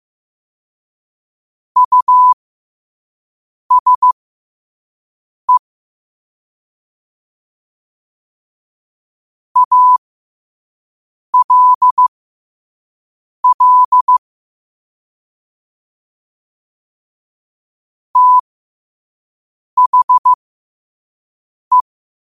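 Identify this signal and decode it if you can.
Morse code, heard as a keyed tone, about 15 wpm, spaced as for 5 wpm, 1000 Hz. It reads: USE ALL THE
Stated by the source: Morse quote of the day at 5 wpm.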